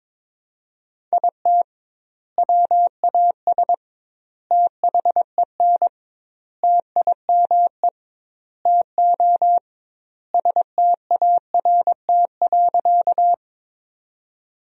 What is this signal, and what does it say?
Morse code, heard as a keyed tone, about 22 wpm, 700 Hz. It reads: IT WAS THEN TIME TO START.
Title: 2x Word-Spacing - Sentences from Top 300 Words - 22wpm